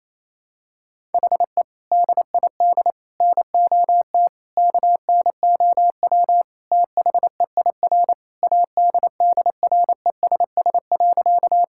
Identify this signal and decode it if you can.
Morse code, heard as a keyed tone, about 28 wpm, 700 Hz. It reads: HE DID NOT KNOW THEIR ADDRESS.
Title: Sentences from Top 1000 Words - Review Part 6 - 28wpm